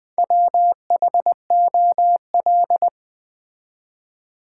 Morse code, heard as a keyed tone, about 20 wpm, 700 Hz. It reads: WHOL
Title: Build A Word - 2 Letters At A Time - 6-Letter Words - 20wpm